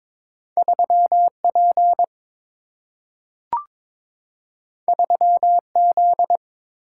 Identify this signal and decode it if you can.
Morse code, heard as a keyed tone, about 22 wpm, 700 Hz. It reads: 3 P 3 Z
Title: Single-Letter Single-Number 22wpm